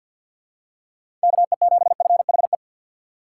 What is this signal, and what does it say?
Morse code, heard as a keyed tone, about 48 wpm, 700 Hz. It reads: KE7FHE
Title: USA Call Signs - 6 Characters - 6x Character Spacing - 50wpm